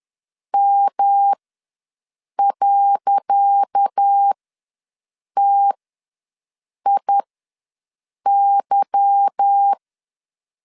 Morse code, heard as a keyed tone, about 10 wpm, spaced as for 6 wpm, 800 Hz. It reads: M.TIY
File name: LEZIONI / 10 SIGNS / 10C DOT ws 10wpm 4213